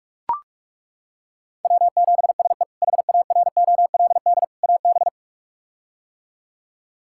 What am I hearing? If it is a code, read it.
Morse code, heard as a keyed tone, about 41 wpm, 700 Hz. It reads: W7SE HAROLD AB
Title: K1USN SST QSO Practice - 45wpm